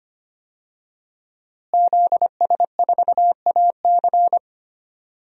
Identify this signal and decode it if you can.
Morse code, heard as a keyed tone, about 25 wpm, 700 Hz. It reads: ZS4AC
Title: DXCC-100 Call Signs - 5 Characters - Repeated 2x - 25wpm